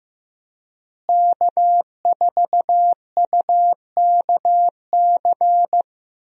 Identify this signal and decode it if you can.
Morse code, heard as a keyed tone, about 15 wpm, 700 Hz. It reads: K4UKC
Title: USA Call Signs - 5 Characters - Repeated 3x - 15wpm